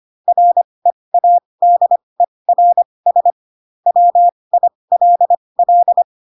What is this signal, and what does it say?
Morse code, heard as a keyed tone, about 25 wpm, spaced as for 20 wpm, 700 Hz. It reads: READERS WILL
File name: Pinocchio / Pinocchio ch1 15 25 ik1hge